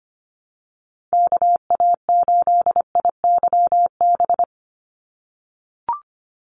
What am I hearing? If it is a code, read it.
Morse code, heard as a keyed tone, about 25 wpm, 700 Hz. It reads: KA8IYB
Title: USA Call Signs 25wpm